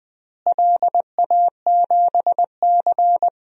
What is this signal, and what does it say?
Morse code, heard as a keyed tone, about 20 wpm, 700 Hz. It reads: LA7C
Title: DXCC-100 Call Signs - 4 Characters - Repeated 2x - 20wpm